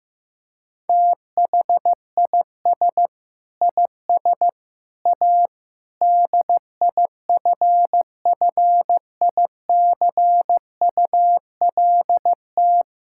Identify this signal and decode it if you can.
Morse code, heard as a keyed tone, about 15 wpm, 700 Hz. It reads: THIS IS A DIFFICULT
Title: Sentences from Top 1000 Words - Review Part 5 - 15wpm